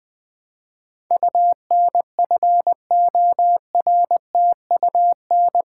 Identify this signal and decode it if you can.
Morse code, heard as a keyed tone, about 20 wpm, 700 Hz. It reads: UNFORTUN
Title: Build A Word - 4 Letters At A Time - 11-Letter Words - 20wpm